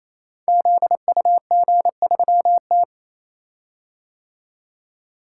Morse code, heard as a keyed tone, about 29 wpm, 700 Hz. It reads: ZUG3T
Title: Code Groups - 5 Letter:Numbers - 28wpm